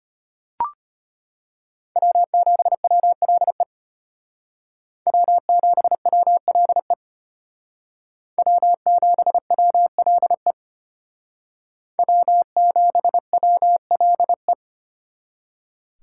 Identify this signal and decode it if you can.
Morse code, heard as a keyed tone, about 30 wpm, 700 Hz. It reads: W7WLE W7WLE W7WLE W7WLE
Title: Speed Racing - USA 1x3 Call Signs 25wpm